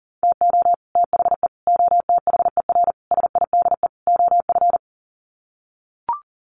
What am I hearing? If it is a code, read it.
Morse code, heard as a keyed tone, about 40 wpm, 700 Hz. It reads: TO THE OTHER SIDE OF
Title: Sets of 5 Words Encore 40wpm